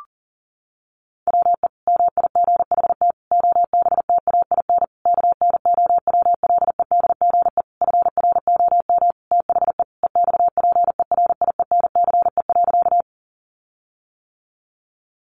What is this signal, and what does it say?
Morse code, heard as a keyed tone, about 40 wpm, 700 Hz. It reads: WE MIGHT OBTAIN KNOWLEDGE FROM THE EXPERIENCE.
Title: Sentences from Top 1300 Words 40wpm